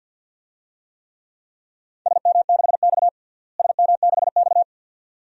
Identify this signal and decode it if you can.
Morse code, heard as a keyed tone, about 46 wpm, 700 Hz. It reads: SM6X SM6X
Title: DXCC-100 Call Signs - 4 Characters - Repeated 2x - 50wpm